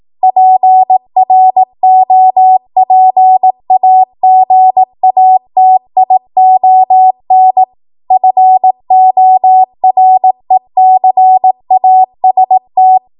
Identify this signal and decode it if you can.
Morse code, heard as a keyed tone, about 18 wpm, 750 Hz. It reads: PROPAGATION FORECAST
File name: Morse / 18WPM